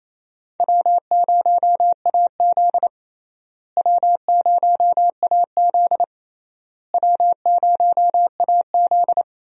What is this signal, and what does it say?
Morse code, heard as a keyed tone, about 28 wpm, 700 Hz. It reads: W0AZ W0AZ W0AZ
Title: USA Call Signs - 4 Characters - Repeated 3x - 28wpm